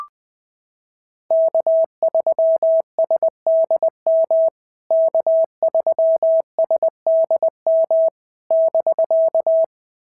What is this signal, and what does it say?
Morse code, heard as a keyed tone, about 21 wpm, 650 Hz. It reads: K3SDM K3SDM <BK>
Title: POTA QSO Practice - 20wpm